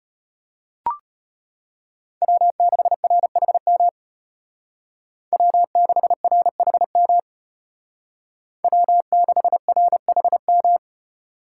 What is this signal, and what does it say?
Morse code, heard as a keyed tone, about 31 wpm, 700 Hz. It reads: W6RHM W6RHM W6RHM
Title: Speed Racing - USA 1x3 Call Signs 25wpm